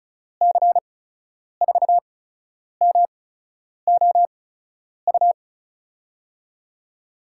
Morse code, heard as a keyed tone, about 34 wpm, spaced as for 10 wpm, 700 Hz. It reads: C4MOU
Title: USA Call Signs - 6 Characters - 8x Character Spacing - 35wpm